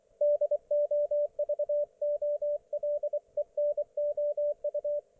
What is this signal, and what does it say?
Morse code, heard as a keyed tone, about 24 wpm, 590 Hz, with radio noise. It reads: DOVOLENOU
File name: Archiv / CUC212 / ok1fhi cul212